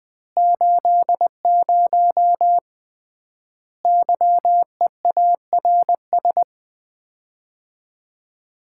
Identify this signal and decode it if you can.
Morse code, heard as a keyed tone, about 21 wpm, 700 Hz. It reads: 80 YEARS
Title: Common QSO Phrases - 3x Word-Spacing - 20wpm